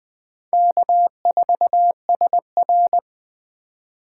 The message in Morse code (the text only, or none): K4SR